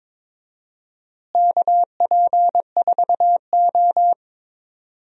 Morse code, heard as a keyed tone, about 22 wpm, 700 Hz. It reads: KP4O
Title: DXCC-100 Call Signs - 4 Characters - Repeated 3x - 22wpm